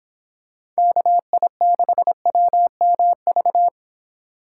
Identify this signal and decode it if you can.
Morse code, heard as a keyed tone, about 26 wpm, 700 Hz. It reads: KI6WMV
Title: Speed Racing - USA 2x3 Call Signs 22wpm